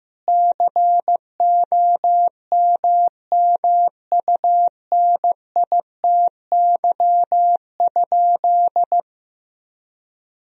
Morse code, heard as a keyed tone, about 15 wpm, 700 Hz. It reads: COMMUNITY?
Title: Sentences from Top 1000 Words - Review Part 4 - 15wpm